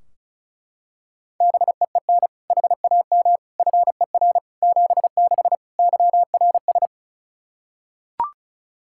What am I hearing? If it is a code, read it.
Morse code, heard as a keyed tone, about 36 wpm, 700 Hz. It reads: BEEN HAM FER 76 YRS
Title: Practice QSO - Exchange 3 - More Info - 35wpm